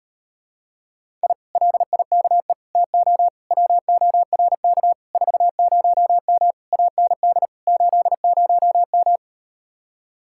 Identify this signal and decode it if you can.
Morse code, heard as a keyed tone, about 38 wpm, 700 Hz. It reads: I LIKE TO WORK 40M AND 80M